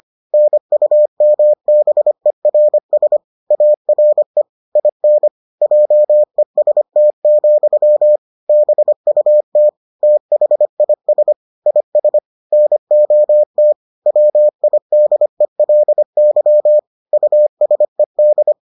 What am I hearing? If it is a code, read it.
Morse code, heard as a keyed tone, about 25 wpm, 600 Hz. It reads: NUMBERS ARE IN JEST, BUT THIS IS NOT WIDELY USED